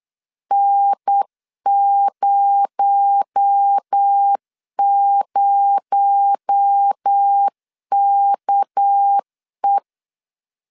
Morse code, heard as a keyed tone, about 8 wpm, 800 Hz. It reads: N00KE
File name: LEZIONI / 6 PKX0 / 6 PKX0 8wpm 3341